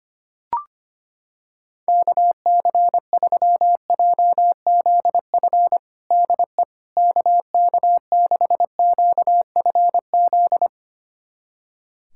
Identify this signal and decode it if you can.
Morse code, heard as a keyed tone, about 25 wpm, 700 Hz. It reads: KC3JZF DE KK6QFZ
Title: USA Callsign DE Callsign 25wpm